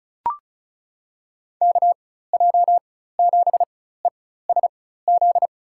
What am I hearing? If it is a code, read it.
Morse code, heard as a keyed tone, about 34 wpm, spaced as for 16 wpm, 700 Hz. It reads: KJ7ESZ
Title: USA Call Signs - 6 Characters - 4x Character Spacing - 35wpm